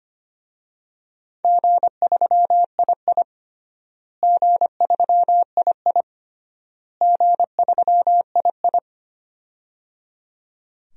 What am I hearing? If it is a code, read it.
Morse code, heard as a keyed tone, about 25 wpm, 700 Hz. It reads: G3II G3II G3II
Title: DXCC-100 Call Signs - 4 Characters - Repeated 3x - 25wpm